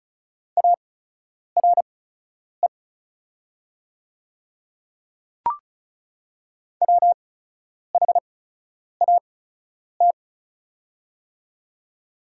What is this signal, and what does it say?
Morse code, heard as a keyed tone, about 36 wpm, spaced as for 10 wpm, 690 Hz. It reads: ARE WHAT
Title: 8x Character Spacing - Top 200 Words - 35wpm